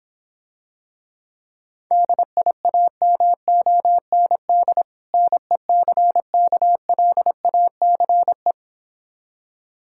Morse code, heard as a keyed tone, about 26 wpm, 700 Hz. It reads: DIAMOND NECKLACE